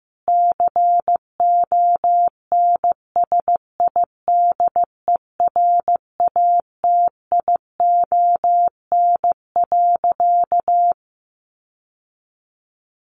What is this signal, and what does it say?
Morse code, heard as a keyed tone, about 15 wpm, 700 Hz. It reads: CONSIDERATION.
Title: Sentences from Top 1500 Words 15wpm